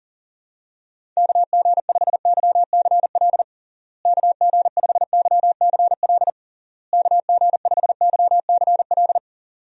Morse code, heard as a keyed tone, about 40 wpm, 700 Hz. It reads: KG5YCL KG5YCL KG5YCL
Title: USA Call Signs - 6 Characters - Repeated 3x - 40wpm